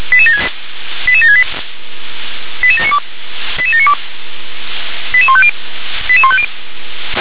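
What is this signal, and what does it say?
Five-tone selective call (ZVEI), heard as quick runs of five tones, tones of about 70 ms. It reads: 88768 88792 88792 88150 88150
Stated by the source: Received on an SDR, ZVEI five-tone selective call, code 88792.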